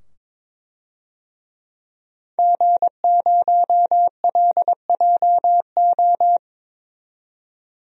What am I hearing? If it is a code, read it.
Morse code, heard as a keyed tone, about 22 wpm, 700 Hz. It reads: G0LJO
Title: DXCC-100 Call Signs - 5 Characters - 22wpm